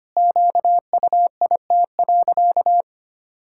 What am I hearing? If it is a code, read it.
Morse code, heard as a keyed tone, about 25 wpm, 700 Hz. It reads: QUIT.
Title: Pangram Challenge v2 - 25wpm